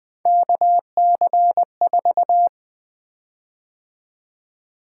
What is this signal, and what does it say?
Morse code, heard as a keyed tone, about 21 wpm, 700 Hz. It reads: KC4
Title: Three Letter-Number Combinations - 20wpm